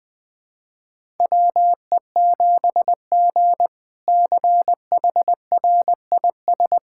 Morse code, heard as a keyed tone, about 20 wpm, 700 Hz. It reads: WE7G CHRIS
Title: K1USN SST QSO Practice - 20wpm